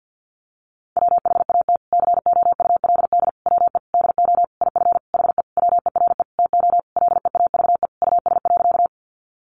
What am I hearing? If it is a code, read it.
Morse code, heard as a keyed tone, about 50 wpm, 700 Hz. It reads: WHAT COULD WE DO IF HE WERE TO LEAVE US.
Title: Sentences from Top 400 Words 50wpm